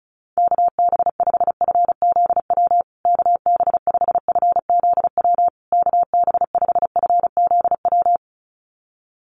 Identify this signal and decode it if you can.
Morse code, heard as a keyed tone, about 34 wpm, 700 Hz. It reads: KB5FZW KB5FZW KB5FZW
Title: USA Call Signs Repeated 3x 35wpm